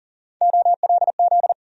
Morse code, heard as a keyed tone, about 40 wpm, 700 Hz. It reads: OLZ
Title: USA Call Sign Postfixes - 3 Letters 40wpm